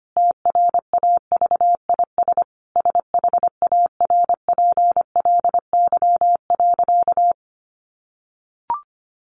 Morse code, heard as a keyed tone, about 25 wpm, 700 Hz. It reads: TRAVIS SHARPLY.
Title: A Sound Of Thunder 25wpm